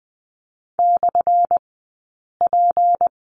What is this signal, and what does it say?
Morse code, heard as a keyed tone, about 20 wpm, spaced as for 8 wpm, 700 Hz. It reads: /P
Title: 177 Lesson 29d - Slash - Two Characters 20wpm